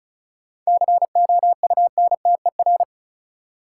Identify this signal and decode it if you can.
Morse code, heard as a keyed tone, about 34 wpm, 700 Hz. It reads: COUNTER
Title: Seven Letter Words - 35wpm